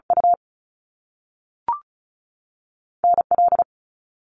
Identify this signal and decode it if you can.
Morse code, heard as a keyed tone, about 36 wpm, 700 Hz. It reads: U N L
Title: Top Two Letter Combinations in Words 35wpm